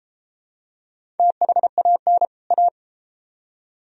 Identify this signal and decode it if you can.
Morse code, heard as a keyed tone, about 31 wpm, 700 Hz. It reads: THAN A